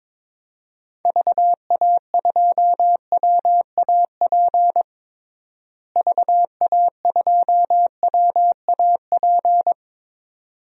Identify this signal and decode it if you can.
Morse code, heard as a keyed tone, about 21 wpm, 700 Hz. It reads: VA2WAP VA2WAP
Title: DXCC-100 Call Signs - 6 Characters - Repeated 2x - 22wpm